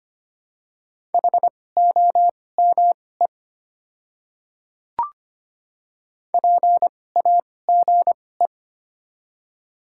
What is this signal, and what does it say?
Morse code, heard as a keyed tone, about 25 wpm, spaced as for 18 wpm, 700 Hz. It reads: HOME PAGE
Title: 2x Character Spacing - Top 300 Words - 25wpm